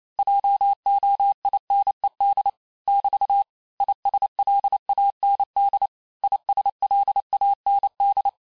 Morse code, heard as a keyed tone, about 28 wpm, 800 Hz. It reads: JOINED <BT> ISLAND ISLAND
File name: cours-du-mercredi / audios / cours-865-teste-les-mots-vitesse-27mpm